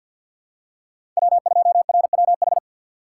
Morse code, heard as a keyed tone, about 48 wpm, 700 Hz. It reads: W2RWH
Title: USA Call Signs - 5 Characters - 4x Character Spacing - 50wpm